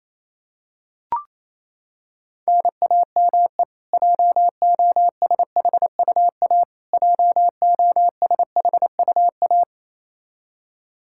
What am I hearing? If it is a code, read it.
Morse code, heard as a keyed tone, about 28 wpm, 700 Hz. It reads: NAME JOSHUA JOSHUA
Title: Practice QSO - Exchange 1 - Beginning - 28wpm